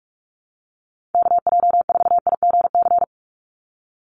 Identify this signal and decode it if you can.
Morse code, heard as a keyed tone, about 46 wpm, 700 Hz. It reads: KJ4IGC
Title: USA Call Signs 45wpm